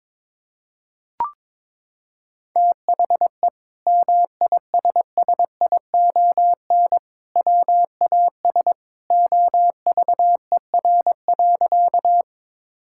Morse code, heard as a keyed tone, about 22 wpm, 700 Hz. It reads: THE MISSION WAS OVER.
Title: Sentences from Top 1600 Words 22wpm